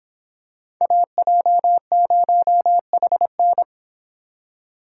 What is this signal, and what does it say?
Morse code, heard as a keyed tone, about 26 wpm, 700 Hz. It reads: AJ0HN